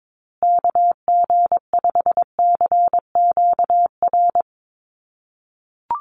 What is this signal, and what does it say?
Morse code, heard as a keyed tone, about 22 wpm, 700 Hz. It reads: KG5CQR
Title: USA Call Signs Repeated 3x 22wpm